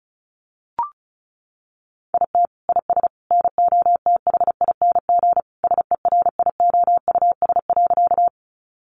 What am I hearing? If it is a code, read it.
Morse code, heard as a keyed tone, about 34 wpm, 700 Hz. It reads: IT IS NOTHING SERIOUS.
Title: Sentences from Top 1100 Words 35wpm